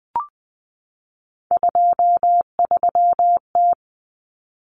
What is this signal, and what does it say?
Morse code, heard as a keyed tone, about 20 wpm, 700 Hz. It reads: 23T
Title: Common Cut Numbers - 3-Digit 20wpm